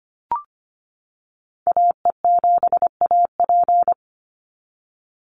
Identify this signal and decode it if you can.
Morse code, heard as a keyed tone, about 25 wpm, 700 Hz. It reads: AE7AP
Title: NA SOTA Chaser Callsigns 25wpm